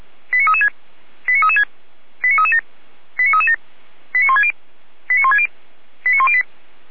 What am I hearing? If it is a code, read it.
Five-tone selective call (ZVEI), heard as quick runs of five tones, tones of about 70 ms. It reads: 89336 89336 89337 89337 89160 89160 89108